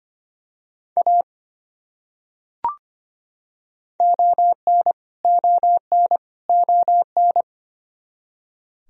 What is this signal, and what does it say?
Morse code, heard as a keyed tone, about 25 wpm, 700 Hz. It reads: A ON ON ON
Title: Top 100 Words Repeated 25wpm